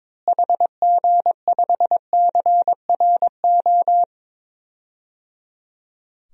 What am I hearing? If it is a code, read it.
Morse code, heard as a keyed tone, about 22 wpm, 700 Hz. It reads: HG5CRO
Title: DXCC-100 Call Signs - 6 Characters - Repeated 3x - 22wpm